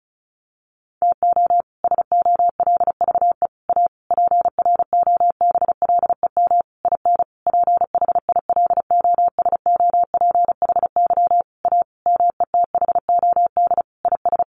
Morse code, heard as a keyed tone, about 34 wpm, 700 Hz. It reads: TO SOLVE A PROBLEM IN PHILOSOPHY A METHOD IS
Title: Sentences from Top 1400 Words 35wpm